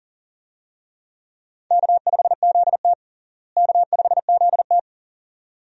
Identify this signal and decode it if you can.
Morse code, heard as a keyed tone, about 37 wpm, 700 Hz. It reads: K5ZT K5ZT
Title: USA Call Signs - 4 Characters - Repeated 2x - 40wpm